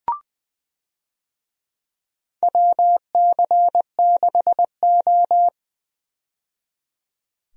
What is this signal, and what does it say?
Morse code, heard as a keyed tone, about 20 wpm, 700 Hz. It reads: WC6O
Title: USA Call Signs - 4 Characters - 20wpm